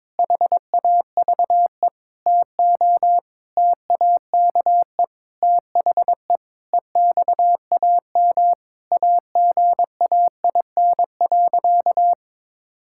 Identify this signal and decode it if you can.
Morse code, heard as a keyed tone, about 21 wpm, 700 Hz. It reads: HAVE TO TAKE THE EXAM AGAIN.
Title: Sentences from Top 1000 Words - Review Part 5 - 22wpm